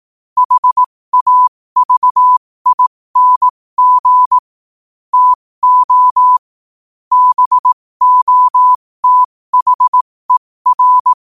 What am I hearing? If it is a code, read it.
Morse code, heard as a keyed tone, about 18 wpm, spaced as for 15 wpm, 1000 Hz. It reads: HAVING TO BOTHER
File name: qotdfiles / 15wpm